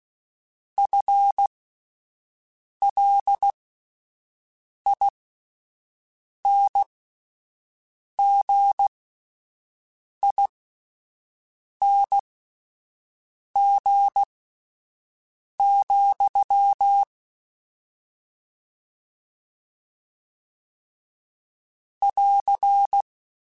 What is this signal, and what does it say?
Morse code, heard as a keyed tone, about 16 wpm, spaced as for 5 wpm, 800 Hz. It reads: FLINGING, <AR>
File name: Combinations / Combinations-mp3s / Combinations@5